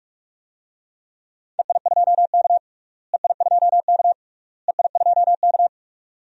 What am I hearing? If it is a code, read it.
Morse code, heard as a keyed tone, about 41 wpm, 700 Hz. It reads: EI2K EI2K EI2K
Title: DXCC-100 Call Signs - 4 Characters - Repeated 3x - 45wpm